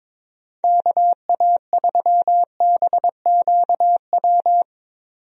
Morse code, heard as a keyed tone, about 22 wpm, 700 Hz. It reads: KA3BQW
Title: USA Call Signs - 6 Characters - 22wpm